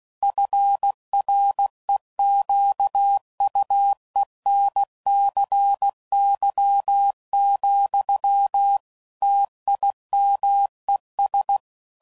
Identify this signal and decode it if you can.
Morse code, heard as a keyed tone, about 16 wpm, 800 Hz. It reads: FREQUENCY, TIMES
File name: CW Code Groups @15WPM / Technical Subjects / XC